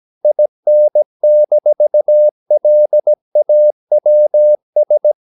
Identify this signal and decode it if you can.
Morse code, heard as a keyed tone, about 17 wpm, 600 Hz. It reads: IN-LAWS
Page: Romanen Never af Ken Follet på engelsk.